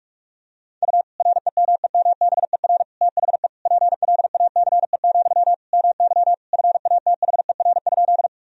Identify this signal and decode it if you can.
Morse code, heard as a keyed tone, about 46 wpm, 700 Hz. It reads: U REMEMBER THE PLACE, MY FATHER?
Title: Sentences from Top 400 Words and Abbreviations 45wpm